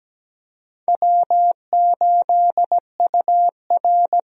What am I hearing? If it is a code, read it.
Morse code, heard as a keyed tone, about 17 wpm, 700 Hz. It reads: W8UR
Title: USA Call Signs - 4 Characters - Repeated 2x - 17wpm